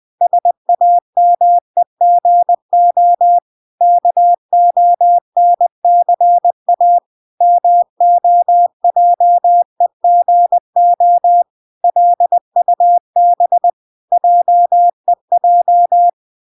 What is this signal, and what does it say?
Morse code, heard as a keyed tone, about 20 wpm, 700 Hz. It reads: SAMEGO KONCA MOJEGO LUB JEJ